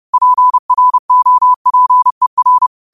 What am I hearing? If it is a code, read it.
Morse code, heard as a keyed tone, about 30 wpm, 1000 Hz. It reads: PROPER
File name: qotdfiles / 30wpm